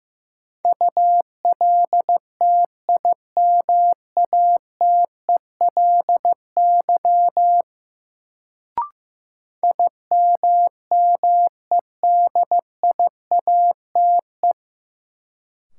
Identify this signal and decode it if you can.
Morse code, heard as a keyed tone, about 15 wpm, 700 Hz. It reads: ULTIMATELY IMMEDIATE
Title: Sentences from Top 1200 Words - New Word Drill - 15wpm